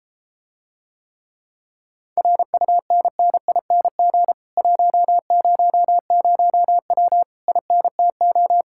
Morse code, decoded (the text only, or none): RUNNING 100W INTO